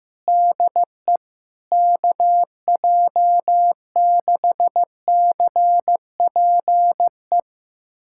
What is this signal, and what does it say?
Morse code, heard as a keyed tone, about 15 wpm, 700 Hz. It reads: DE KJ6CPE
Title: USA Callsign DE Callsign 15wpm